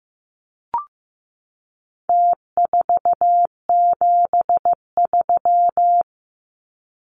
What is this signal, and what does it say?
Morse code, heard as a keyed tone, about 15 wpm, 700 Hz. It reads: T473
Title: Numbers - Contest Style - 4-Digit 15wpm